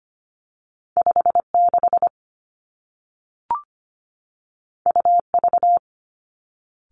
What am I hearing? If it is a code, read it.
Morse code, heard as a keyed tone, about 25 wpm, 700 Hz. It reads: 5 6 U V
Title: Two Challenging Letter-Number Combinations 25wpm